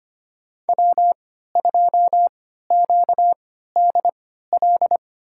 Morse code, heard as a keyed tone, about 25 wpm, spaced as for 14 wpm, 700 Hz. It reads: W2QDL
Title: USA Call Signs - 5 Characters - 3x Character Spacing - 25wpm